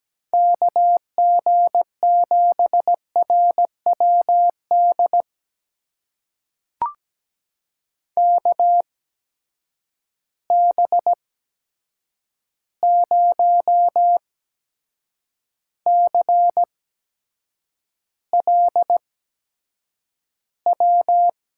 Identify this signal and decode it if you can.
Morse code, heard as a keyed tone, about 17 wpm, spaced as for 7 wpm, 700 Hz. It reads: KG7RWD KB0CLW
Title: USA Call Signs - 6 Characters - 8x Character Spacing - 17wpm